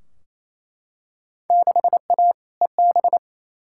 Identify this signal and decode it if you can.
Morse code, heard as a keyed tone, about 28 wpm, 700 Hz. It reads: 6A EB
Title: Field Day Exchanges Encore - 28wpm